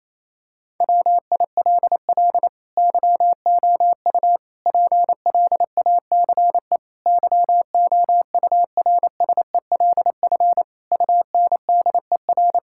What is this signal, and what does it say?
Morse code, heard as a keyed tone, about 28 wpm, 700 Hz. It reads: WILL YOU PLACE YOURSELF UNDER
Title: Sentences from Top 1600 Words 28wpm